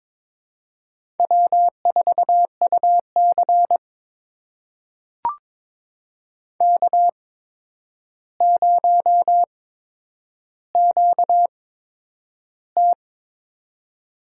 Morse code, heard as a keyed tone, about 22 wpm, spaced as for 9 wpm, 700 Hz. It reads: W4UC K0QT
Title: USA Call Signs - 4 Characters - 8x Character Spacing - 22wpm